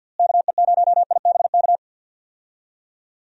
Morse code, heard as a keyed tone, about 52 wpm, 700 Hz. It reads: KE0IBK
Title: USA Call Signs - 6 Characters - Repeated 2x - 50wpm